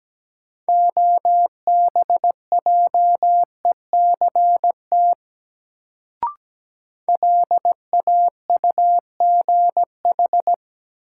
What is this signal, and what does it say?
Morse code, heard as a keyed tone, about 17 wpm, 700 Hz. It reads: OBJECT LAUGH
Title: Sentences from Top 500 Words - New Word Drill - 17wpm